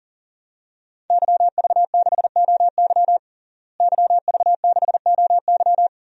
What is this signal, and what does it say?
Morse code, heard as a keyed tone, about 37 wpm, 700 Hz. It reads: YV6OY YV6OY
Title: DXCC-100 Call Signs - 5 Characters - Repeated 2x - 40wpm